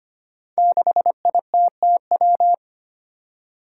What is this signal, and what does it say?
Morse code, heard as a keyed tone, about 25 wpm, 700 Hz. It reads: 6ITTW
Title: Five Challenging Letter-Number Combinations 25wpm